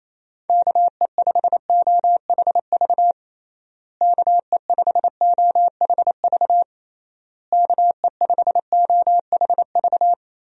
Morse code, heard as a keyed tone, about 28 wpm, 700 Hz. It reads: KE5OHV KE5OHV KE5OHV
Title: USA Call Signs - 6 Characters - Repeated 3x - 28wpm